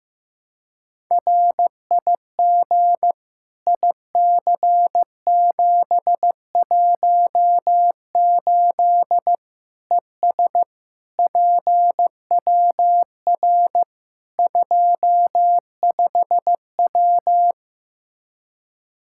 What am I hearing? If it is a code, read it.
Morse code, heard as a keyed tone, about 16 wpm, 700 Hz. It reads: RIG IC718 ES PWR 25W
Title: K1USN SST QSO Practice 15wpm